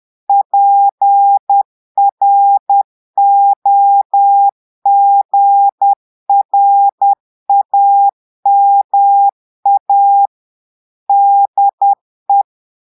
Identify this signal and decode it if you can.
Morse code, heard as a keyed tone, about 10 wpm, 800 Hz. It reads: PROGRAMA DE
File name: Portugais - texte en clair 1 - vitesse - 10 mots minute